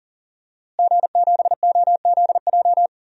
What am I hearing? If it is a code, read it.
Morse code, heard as a keyed tone, about 37 wpm, 700 Hz. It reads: G7OZJ
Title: DXCC-100 Call Signs - 5 Characters - Repeated 3x - 40wpm